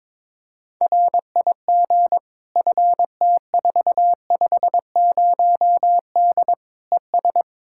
Morse code, heard as a keyed tone, about 22 wpm, 700 Hz. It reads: RIG FT450D ES
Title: Practice QSO - Exchange 2 - Info - 22wpm